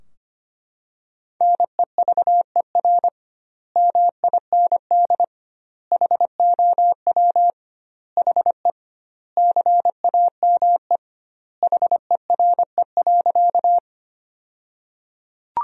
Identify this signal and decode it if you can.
Morse code, heard as a keyed tone, about 25 wpm, 700 Hz. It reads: NEVER MIND HOW HE CAME HERE.
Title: 2x Word-Spacing - Sentences from Top 500 Words - 25wpm